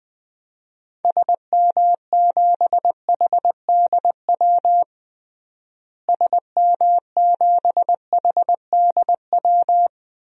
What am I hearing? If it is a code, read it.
Morse code, heard as a keyed tone, about 20 wpm, 700 Hz. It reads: SM7HDW SM7HDW
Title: DXCC-100 Call Signs - 6 Characters - Repeated 2x - 20wpm